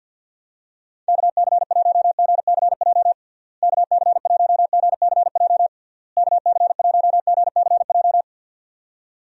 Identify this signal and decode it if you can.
Morse code, heard as a keyed tone, about 51 wpm, 700 Hz. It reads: KC1GCJ KC1GCJ KC1GCJ
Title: USA Call Signs - 6 Characters - Repeated 3x - 50wpm